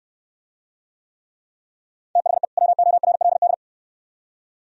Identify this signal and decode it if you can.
Morse code, heard as a keyed tone, about 69 wpm, 700 Hz. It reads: THE WORLD